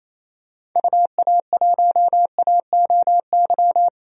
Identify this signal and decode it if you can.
Morse code, heard as a keyed tone, about 28 wpm, 700 Hz. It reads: UA1AOY
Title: DXCC-100 Call Signs - 6 Characters - 28wpm